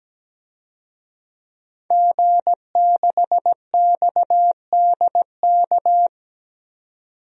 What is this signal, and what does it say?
Morse code, heard as a keyed tone, about 17 wpm, 700 Hz. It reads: G6XDK
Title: DXCC-100 Call Signs - 5 Characters - 17wpm